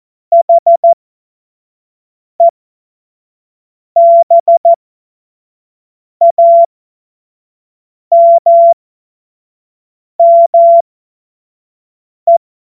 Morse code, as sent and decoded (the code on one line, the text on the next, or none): .... -... .- -- -- .
HEBAMME